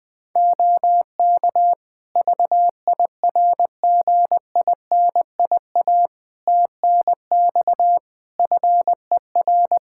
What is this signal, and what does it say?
Morse code, heard as a keyed tone, about 20 wpm, 700 Hz. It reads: OK VIRGINIA TNX FER
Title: Practice QSO - Exchange 4 - Ending - 20wpm